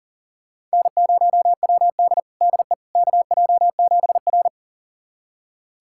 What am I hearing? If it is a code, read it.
Morse code, heard as a keyed tone, about 39 wpm, 700 Hz. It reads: N0WD DE KJ7R
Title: USA Callsign DE Callsign - 4 Characters - 40wpm